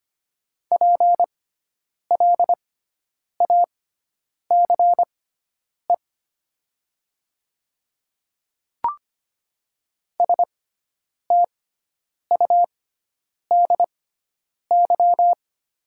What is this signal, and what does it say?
Morse code, heard as a keyed tone, about 25 wpm, spaced as for 9 wpm, 700 Hz. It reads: PLACE STUDY
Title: 6x Character Spacing - Top 300 Words - 25wpm